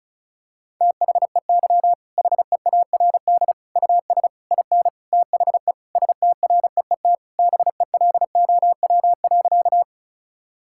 Morse code, heard as a keyed tone, about 34 wpm, 700 Hz. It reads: THEY HEARD US IN THE STREET BELOW.
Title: Sentences from Top 1000 Words - Review Part 5 - 35wpm